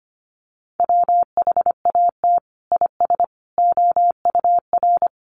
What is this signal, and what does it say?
Morse code, heard as a keyed tone, about 25 wpm, 700 Hz. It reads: WHAT IS OUR
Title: Sentences from Top 1200 Words 25wpm